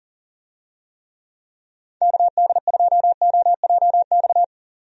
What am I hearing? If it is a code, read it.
Morse code, heard as a keyed tone, about 40 wpm, 700 Hz. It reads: KD2OJX
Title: USA Call Signs - 6 Characters - 40wpm